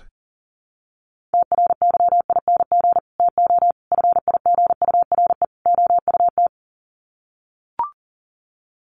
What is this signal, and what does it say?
Morse code, heard as a keyed tone, about 40 wpm, 700 Hz. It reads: TRYING TO FIGURE OUT
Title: Sets of 4 Words 40wpm